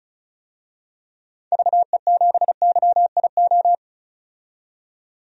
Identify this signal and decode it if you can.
Morse code, heard as a keyed tone, about 36 wpm, 700 Hz. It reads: VE7YIO
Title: DXCC-100 Call Signs - 6 Characters - 35wpm